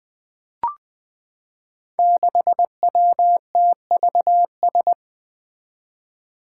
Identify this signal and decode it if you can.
Morse code, heard as a keyed tone, about 20 wpm, 700 Hz. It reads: 6WTVS
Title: Five Challenging Letter-Number Combinations 20wpm